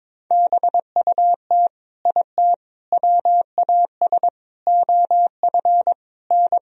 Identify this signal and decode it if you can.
Morse code, heard as a keyed tone, about 22 wpm, 700 Hz. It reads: BUT IT WAS OF N
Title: Sentences from Top 100 Words and Abbreviations 22wpm